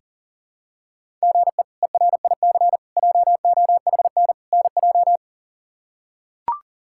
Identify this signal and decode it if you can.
Morse code, heard as a keyed tone, about 40 wpm, 700 Hz. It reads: GE ERIC JOHN NJ
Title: K1USN SST QSO Practice - 40wpm